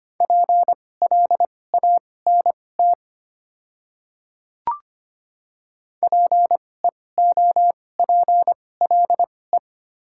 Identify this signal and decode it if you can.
Morse code, heard as a keyed tone, about 25 wpm, spaced as for 18 wpm, 700 Hz. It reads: PLANT PEOPLE
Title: 2x Character Spacing - Top 300 Words - 25wpm